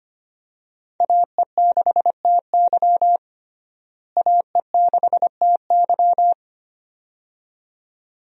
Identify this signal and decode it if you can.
Morse code, heard as a keyed tone, about 25 wpm, 700 Hz. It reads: AE6TY AE6TY
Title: USA Call Signs - 5 Characters - Repeated 2x - 25wpm